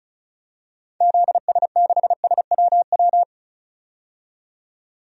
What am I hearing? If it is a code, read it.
Morse code, heard as a keyed tone, about 36 wpm, 700 Hz. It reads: ZS6SWW
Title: DXCC-100 Call Signs - 6 Characters - 35wpm